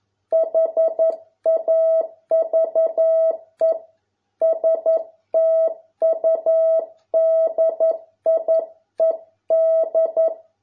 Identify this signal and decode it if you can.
Morse code, heard as a keyed tone, about 11 wpm, 650 Hz. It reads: HAVE STUDIED